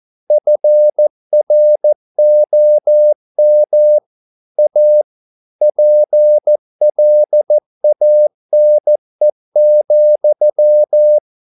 Morse code, heard as a keyed tone, about 14 wpm, 600 Hz. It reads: FROM A PLANE,